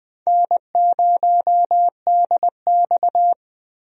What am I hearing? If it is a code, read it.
Morse code, heard as a keyed tone, about 20 wpm, 700 Hz. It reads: N0DX
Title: USA Call Signs - 4 Characters - Repeated 2x - 20wpm